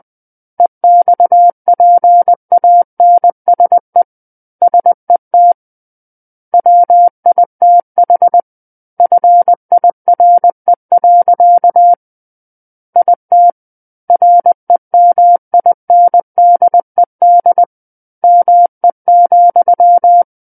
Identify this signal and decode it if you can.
Morse code, heard as a keyed tone, about 20 wpm, 700 Hz. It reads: EXPANSE SET WITH FIRE. IT REMINDED ME,
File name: 20 WPM